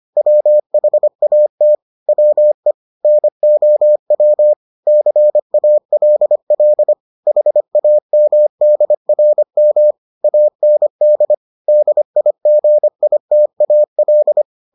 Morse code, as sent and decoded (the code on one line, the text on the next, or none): .-- .... .- - / .-- . / -. --- .-- / -.-. .- .-.. .-.. / .... .- -- -.. .-. -- / .- -. -.. / -.. .. --. .. - .- .-..
WHAT WE NOW CALL HAMDRM AND DIGITAL